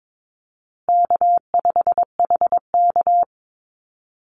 Morse code, heard as a keyed tone, about 22 wpm, 700 Hz. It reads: K5HK
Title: Call Signs Repeated 2x - 22wpm